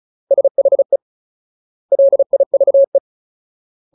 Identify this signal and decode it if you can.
Morse code, heard as a keyed tone, about 34 wpm, 550 Hz. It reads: SHE LIVE
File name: Ultimate Word List / 35WPM Ultimate